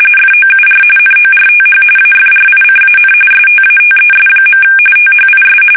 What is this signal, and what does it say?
RTTY, about 45.45 baud. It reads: FOUR NORTH AFRICAN STATES. THIS
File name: fsk850 / usnavy1